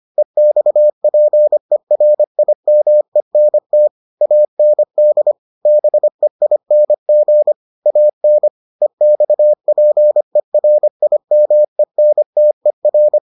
Morse code, heard as a keyed tone, about 25 wpm, 600 Hz. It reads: EXPERIMENT AND BEING AN EXPERIMENTER